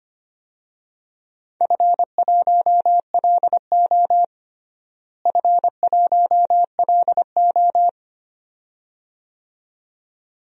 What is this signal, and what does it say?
Morse code, heard as a keyed tone, about 25 wpm, 700 Hz. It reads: F1LO F1LO
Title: DXCC-100 Call Signs - 4 Characters - Repeated 2x - 25wpm